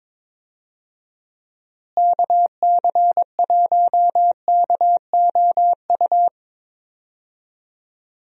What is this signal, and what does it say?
Morse code, heard as a keyed tone, about 21 wpm, 700 Hz. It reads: KC1KOU
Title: USA Call Signs - 6 Characters - Repeated 3x - 22wpm